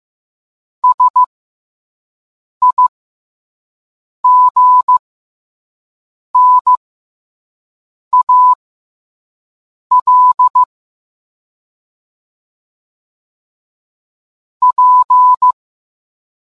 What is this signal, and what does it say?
Morse code, heard as a keyed tone, about 15 wpm, spaced as for 5 wpm, 1000 Hz. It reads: SIGNAL P